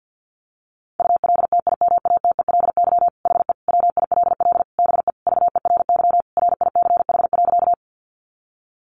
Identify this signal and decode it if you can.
Morse code, heard as a keyed tone, about 50 wpm, 700 Hz. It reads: ULTIMATELY HE WILL BE VERY RICH.
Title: Sentences from Top 1200 Words 50wpm